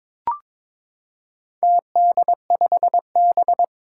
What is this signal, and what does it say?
Morse code, heard as a keyed tone, about 22 wpm, 700 Hz. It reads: TD5B
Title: Four Challenging Letter-Number Combinations 22wpm